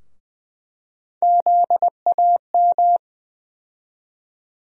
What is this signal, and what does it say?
Morse code, heard as a keyed tone, about 21 wpm, 700 Hz. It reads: ZAM
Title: USA Call Sign Postfixes - 3 Letters 20wpm